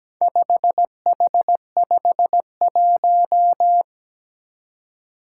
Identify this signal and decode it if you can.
Morse code, heard as a keyed tone, about 17 wpm, 700 Hz. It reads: 5H51
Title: Four Challenging Letter-Number Combinations 17wpm